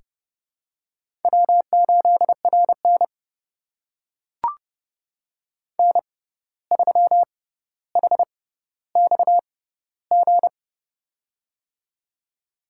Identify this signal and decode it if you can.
Morse code, heard as a keyed tone, about 30 wpm, spaced as for 14 wpm, 700 Hz. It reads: W8RN N3HXG
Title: USA Call Signs - 6x Character Spacing - 30wpm